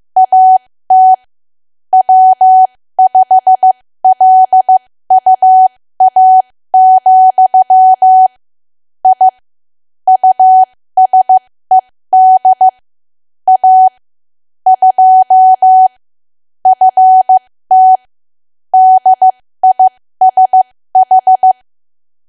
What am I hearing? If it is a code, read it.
Morse code, heard as a keyed tone, about 15 wpm, spaced as for 13 wpm, 750 Hz. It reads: AT W5LUA, I USED A 2 FT DISH
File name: cwa / W1AW-Level2 / 13WPM